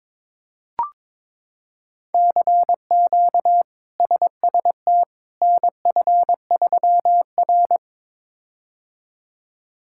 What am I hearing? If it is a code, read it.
Morse code, heard as a keyed tone, about 22 wpm, 700 Hz. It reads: CQ SST NF3R
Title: K1USN SST QSO Practice - 22wpm